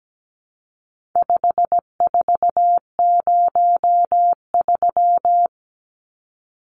Morse code, heard as a keyed tone, about 17 wpm, 700 Hz. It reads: 5403
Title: Numbers - 4-Digit 17wpm